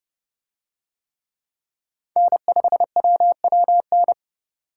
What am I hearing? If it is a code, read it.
Morse code, heard as a keyed tone, about 29 wpm, 700 Hz. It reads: N5WWN